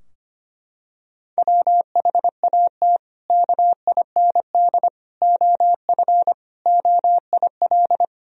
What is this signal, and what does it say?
Morse code, heard as a keyed tone, about 25 wpm, 700 Hz. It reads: WHAT KIND OF OIL
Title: Sentences from Top 1000 Words - Review Part 4 - 25wpm